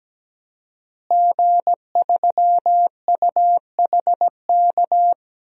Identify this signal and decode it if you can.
Morse code, heard as a keyed tone, about 17 wpm, 700 Hz. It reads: G3UHK
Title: DXCC-100 Call Signs - 5 Characters - Repeated 3x - 17wpm